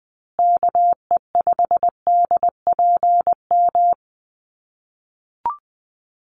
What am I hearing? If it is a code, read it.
Morse code, heard as a keyed tone, about 20 wpm, 700 Hz. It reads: KE5DPM
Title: USA Call Signs Repeated 2x 20wpm